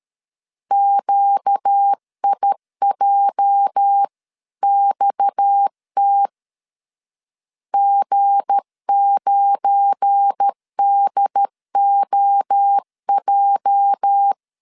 Morse code, heard as a keyed tone, about 13 wpm, 800 Hz. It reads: QIJXT G9DOJ
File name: LEZIONI / 8 YJ45 / 8 YJ 12wpm 2235